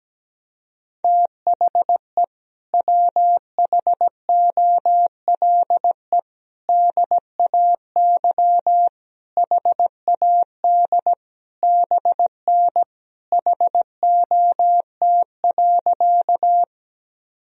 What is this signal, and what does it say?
Morse code, heard as a keyed tone, about 17 wpm, 700 Hz. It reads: THE WHOLE DAY HAD BN HOT.
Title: Sentences from Top 500 Words and Abbreviations 17wpm